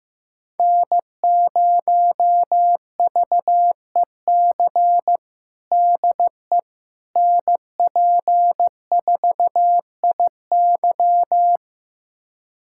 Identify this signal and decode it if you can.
Morse code, heard as a keyed tone, about 15 wpm, 700 Hz. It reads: N0VEC DE NP4IY
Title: USA Callsign DE Callsign - 5 Characters - 15wpm